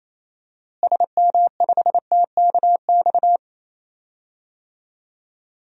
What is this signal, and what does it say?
Morse code, heard as a keyed tone, about 27 wpm, 700 Hz. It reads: SM5TKX
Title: DXCC-100 Call Signs - 6 Characters - Repeated 2x - 28wpm